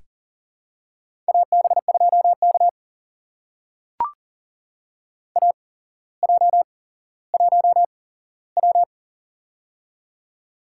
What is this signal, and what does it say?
Morse code, heard as a keyed tone, about 40 wpm, spaced as for 17 wpm, 700 Hz. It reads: AB2K AJ1W
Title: USA Call Signs - 4 Characters - 8x Character Spacing - 40wpm